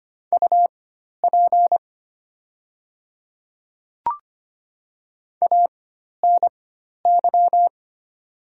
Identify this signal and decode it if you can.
Morse code, heard as a keyed tone, about 25 wpm, spaced as for 12 wpm, 690 Hz. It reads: UP ANY
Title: 4x Character Spacing - Top 200 Words - 25wpm